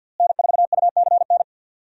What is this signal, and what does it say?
Morse code, heard as a keyed tone, about 52 wpm, 700 Hz. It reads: N4UCN
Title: USA Call Signs - 5 Characters - Repeated 3x - 50wpm